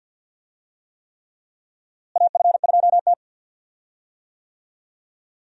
Morse code, heard as a keyed tone, about 46 wpm, 700 Hz. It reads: AF2T
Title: USA Call Signs - 4 Characters - Repeated 3x - 50wpm